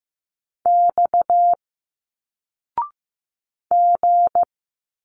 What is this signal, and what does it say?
Morse code, heard as a keyed tone, about 15 wpm, 700 Hz. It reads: X G
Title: Single Letters 15wpm